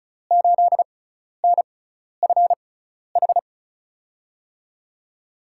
Morse code, heard as a keyed tone, about 36 wpm, spaced as for 12 wpm, 700 Hz. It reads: 8NFH
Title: USA Call Signs - 6 Characters - 6x Character Spacing - 35wpm